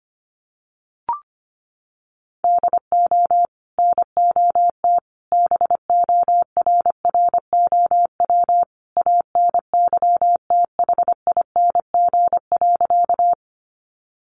Morse code, heard as a keyed tone, about 25 wpm, 700 Hz. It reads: DO NOT BORROW ANYTHING.
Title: Sentences from Top 1500 Words 25wpm